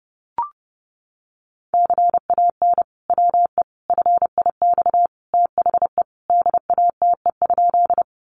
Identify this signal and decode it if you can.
Morse code, heard as a keyed tone, about 30 wpm, 700 Hz. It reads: CAN WE FIX THE DATE?
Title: Sentences from Top 1400 Words 30wpm